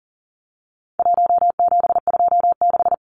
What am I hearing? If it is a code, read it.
Morse code, heard as a keyed tone, about 40 wpm, 700 Hz. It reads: 1726
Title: Numbers - 4-Digit 40wpm